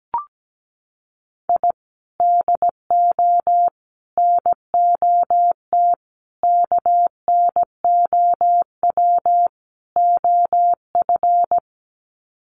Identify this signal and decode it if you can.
Morse code, heard as a keyed tone, about 17 wpm, 700 Hz. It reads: I DO NOT KNOW OF
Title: Sets of 5 Words Encore2 17wpm